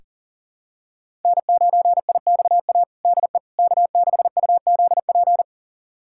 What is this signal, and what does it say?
Morse code, heard as a keyed tone, about 40 wpm, 700 Hz. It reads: N9IXA DE K6UZP
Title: USA Callsign DE Callsign - 5 Characters - 40wpm